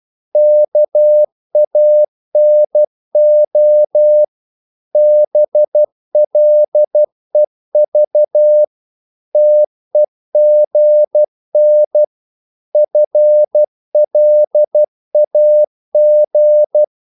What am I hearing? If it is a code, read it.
Morse code, heard as a keyed tone, about 12 wpm, 600 Hz. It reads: KANO BLEV TEGN FLAG
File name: Korte ord DK 12wpm